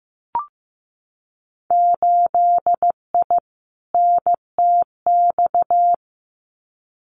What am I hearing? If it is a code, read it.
Morse code, heard as a keyed tone, about 15 wpm, 700 Hz. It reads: 8I NTX
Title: Winter Field Day Exchanges Encore 15wpm